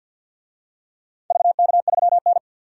Morse code, heard as a keyed tone, about 51 wpm, 700 Hz. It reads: VK3N
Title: DXCC-100 Call Signs - 4 Characters - 50wpm